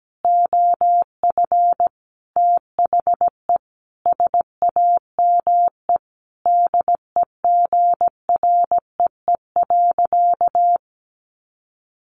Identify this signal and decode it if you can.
Morse code, heard as a keyed tone, about 17 wpm, 700 Hz. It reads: OF THE SAME DEGREE.
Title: Sentences from Top 1500 Words 17wpm